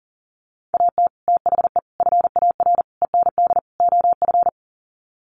Morse code, heard as a keyed tone, about 38 wpm, 700 Hz. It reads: AT THE FAR END OF
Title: Sets of 5 Words 40wpm